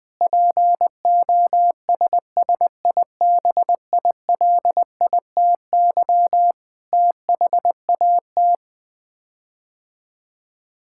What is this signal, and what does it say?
Morse code, heard as a keyed tone, about 20 wpm, 700 Hz. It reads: POSSIBILITY THAT
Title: Sets of 8 Words - 20wpm